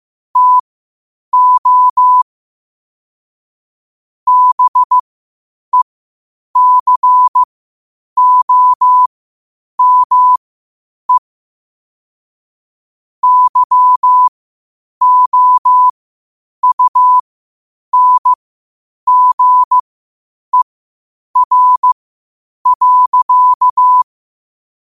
Morse code, heard as a keyed tone, about 15 wpm, spaced as for 8 wpm, 1000 Hz. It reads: TO BECOME YOUNGER.
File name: qotdfiles / 8wpm